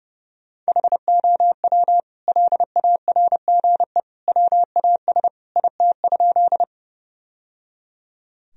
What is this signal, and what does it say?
Morse code, heard as a keyed tone, about 30 wpm, 700 Hz. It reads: HOW LARGE WAS IT?
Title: Sentences from Top 1000 Words - Review Part 1 - 30wpm